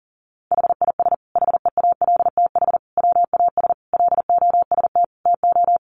Morse code, heard as a keyed tone, about 40 wpm, 700 Hz. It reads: HIS HEALTH WAS LOST TO
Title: Sentences from Top 1400 Words 40wpm